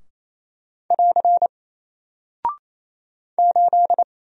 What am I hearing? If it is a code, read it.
Morse code, heard as a keyed tone, about 27 wpm, 700 Hz. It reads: <AR> 8
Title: Single Character (Letter:Number:Punctuation:Prosign) 28wpm